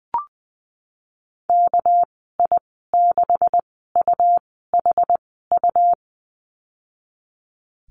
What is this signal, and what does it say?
Morse code, heard as a keyed tone, about 20 wpm, spaced as for 15 wpm, 700 Hz. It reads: KI6UHU
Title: 219 Lesson 35g - Number 0 - Call Signs 20wpm